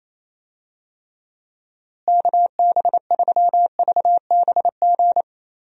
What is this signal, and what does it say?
Morse code, heard as a keyed tone, about 28 wpm, 700 Hz. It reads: KB3VBG
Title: USA Call Signs - 6 Characters - Repeated 3x - 28wpm